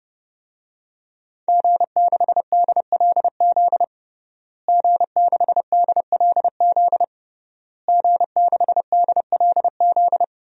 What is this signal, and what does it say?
Morse code, heard as a keyed tone, about 30 wpm, 700 Hz. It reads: G6DLZ G6DLZ G6DLZ
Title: DXCC-100 Call Signs - 5 Characters - Repeated 3x - 30wpm